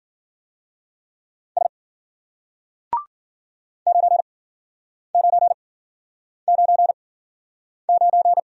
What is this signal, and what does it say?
Morse code, heard as a keyed tone, about 53 wpm, 700 Hz. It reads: S 9 9 9 9